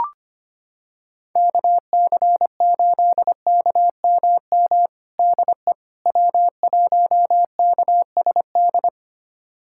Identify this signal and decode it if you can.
Morse code, heard as a keyed tone, about 25 wpm, 700 Hz. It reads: KC8KMM DE W1KSD
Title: USA Callsign DE Callsign 25wpm